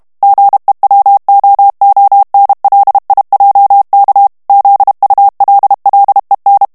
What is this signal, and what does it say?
Morse code, heard as a keyed tone, about 32 wpm, 800 Hz. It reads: GEWOONLIJK ZULLEN
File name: morse mp3 / HOLLANDAIS texte clair 1 vitesse 30 mots minute